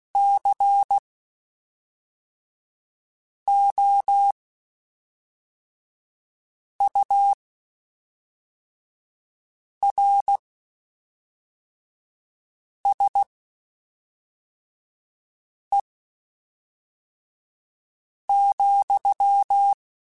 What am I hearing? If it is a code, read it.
Morse code, heard as a keyed tone, about 16 wpm, spaced as for 3 wpm, 800 Hz. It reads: COURSE,
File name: (spaced) / QSOs Spaced-mp3s / WA7QOX 1 @5 sp